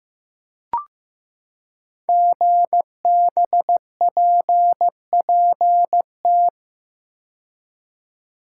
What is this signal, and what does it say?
Morse code, heard as a keyed tone, about 15 wpm, 700 Hz. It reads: GBPPT
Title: Code Groups - 5 Letters - 15wpm